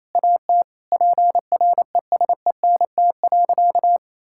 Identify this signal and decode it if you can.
Morse code, heard as a keyed tone, about 28 wpm, 700 Hz. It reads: AT PRESENT.
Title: Sentences from Top 1000 Words - Review Part 5 - 28wpm